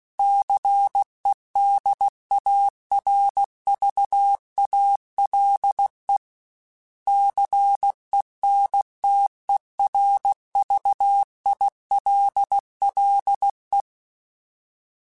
Morse code, heard as a keyed tone, about 16 wpm, 800 Hz. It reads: CEDARVALE CENTERVILLE
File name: fromSMs / FromSMs mp3s / Cities@15